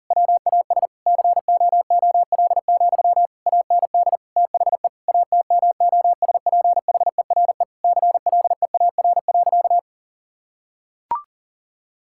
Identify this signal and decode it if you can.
Morse code, heard as a keyed tone, about 40 wpm, 700 Hz. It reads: WAS COOL, AND THE ATMOSPHERE CLEAR.
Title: Sentences from Top 1600 Words 40wpm